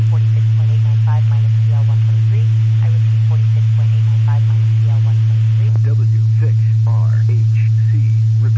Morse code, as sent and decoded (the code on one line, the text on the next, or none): none